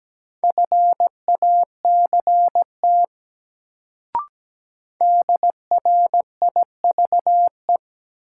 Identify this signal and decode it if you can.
Morse code, heard as a keyed tone, about 17 wpm, 700 Hz. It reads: FACT DRIVE
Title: Sentences from Top 500 Words - New Word Drill - 17wpm